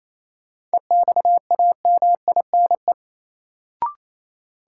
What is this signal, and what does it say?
Morse code, heard as a keyed tone, about 28 wpm, 700 Hz. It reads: EXAMINE
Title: Sentences from Top 1600 Words - New Word Drill Alt - 28wpm